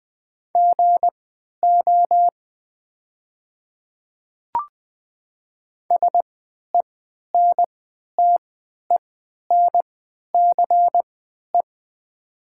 Morse code, heard as a keyed tone, about 20 wpm, spaced as for 11 wpm, 700 Hz. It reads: GO SENTENCE
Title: 3x Character Spacing - Top 200 Words - 20wpm